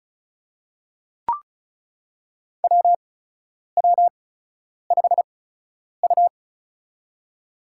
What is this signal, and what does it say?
Morse code, heard as a keyed tone, about 34 wpm, spaced as for 10 wpm, 700 Hz. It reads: WW5U
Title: USA Call Signs - 4 Characters - 8x Character Spacing - 35wpm